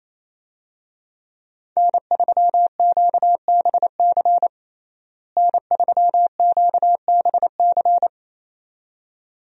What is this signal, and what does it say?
Morse code, heard as a keyed tone, about 28 wpm, 700 Hz. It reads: N3QBC N3QBC
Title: USA Call Signs - 5 Characters - Repeated 2x - 28wpm